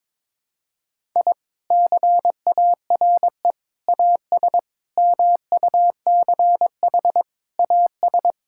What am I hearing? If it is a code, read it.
Morse code, heard as a keyed tone, about 22 wpm, 700 Hz. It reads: I CARE AS MUCH AS
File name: practice / 22wpm / Sentences with 3-Word Phrases - 22wpm